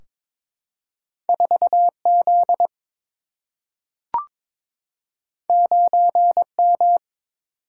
Morse code, heard as a keyed tone, about 22 wpm, 700 Hz. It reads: 4 Z 9 M
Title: Single-Letter Single-Number 22wpm